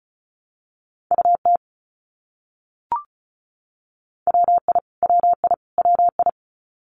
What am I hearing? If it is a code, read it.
Morse code, heard as a keyed tone, about 34 wpm, 700 Hz. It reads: UT WI WI WI
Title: US States Repeated 3x 35wpm